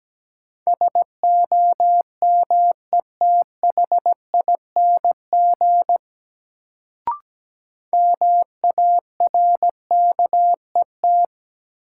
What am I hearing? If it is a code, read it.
Morse code, heard as a keyed tone, about 17 wpm, 700 Hz. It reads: SOMETHING MARKET
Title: Sentences from Top 600 Words - New Word Drill - 17wpm